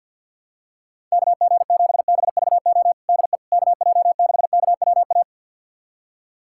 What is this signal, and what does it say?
Morse code, heard as a keyed tone, about 51 wpm, 700 Hz. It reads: KG7BVO DE KJ6KWA
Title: USA Callsign DE Callsign - 6 Characters - 50wpm